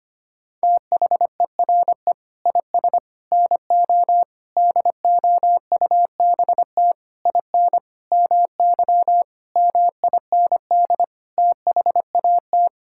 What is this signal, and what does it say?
Morse code, heard as a keyed tone, about 25 wpm, 700 Hz. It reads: THERE IS NO DOUBT IN MY MIND THAT
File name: Sets of 9 Words - 25wpm